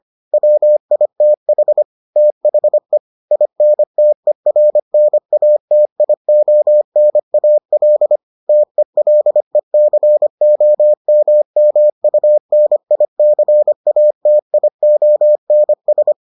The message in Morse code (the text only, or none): WITH THE INTERNATIONAL TELECOMMUNICATIONS